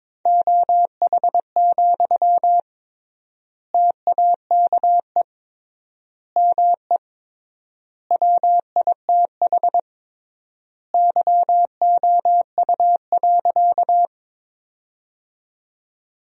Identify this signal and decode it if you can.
Morse code, heard as a keyed tone, about 22 wpm, 700 Hz. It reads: OH, TAKE ME WITH YOU.
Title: 3x Word-Spacing - Sentences from Top 500 Words - 22wpm